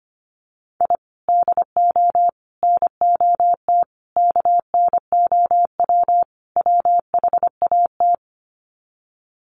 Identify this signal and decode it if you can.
Morse code, heard as a keyed tone, about 25 wpm, 700 Hz. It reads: I DO NOT KNOW WHAT
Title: Sets of 5 Words 25wpm